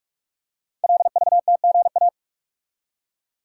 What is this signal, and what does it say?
Morse code, heard as a keyed tone, about 44 wpm, 700 Hz. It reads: LVTGA
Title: Code Groups - 5 Letters - 45wpm